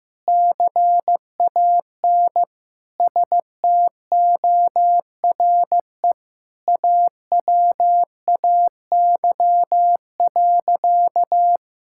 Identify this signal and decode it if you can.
Morse code, heard as a keyed tone, about 15 wpm, 700 Hz. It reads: CAN STORE AWAY.
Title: Sentences from Top 1000 Words - Review Part 5 - 15wpm